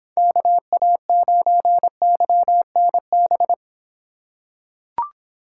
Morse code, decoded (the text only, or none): KA9YNB